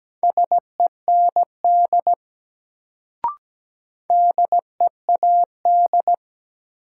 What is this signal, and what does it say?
Morse code, heard as a keyed tone, about 17 wpm, 700 Hz. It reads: SEND DEAD
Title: Sentences from Top 1500 Words - New Word Drill Alt - 17wpm